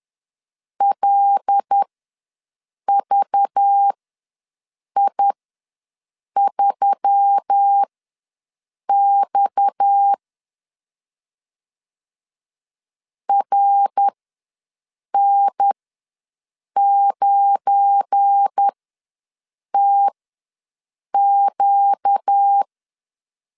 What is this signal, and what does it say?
Morse code, heard as a keyed tone, about 10 wpm, spaced as for 6 wpm, 800 Hz. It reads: LVI3X RN9TQ
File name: LEZIONI / 7 GF9Z / 7 GF9Z ws 10wpm 3919